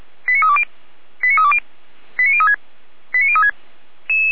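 Five-tone selective call (ZVEI), heard as quick runs of five tones, tones of about 70 ms. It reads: 89320 89320 89036 89036
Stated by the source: ZVEI five-tone call received by SDR.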